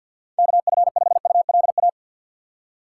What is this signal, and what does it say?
Morse code, heard as a keyed tone, about 52 wpm, 700 Hz. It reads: KF5ULA
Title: USA Call Signs - 6 Characters - 3x Character Spacing - 50wpm